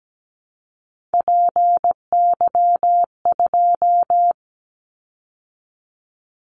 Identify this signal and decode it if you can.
Morse code, heard as a keyed tone, about 17 wpm, 700 Hz. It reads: PY2
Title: Callsign Prefixes for Top 100 DXCC - 17wpm